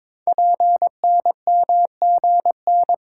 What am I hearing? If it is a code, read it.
Morse code, heard as a keyed tone, about 22 wpm, 700 Hz. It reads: PNMGN
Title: Code Groups - 5 Letters - 22wpm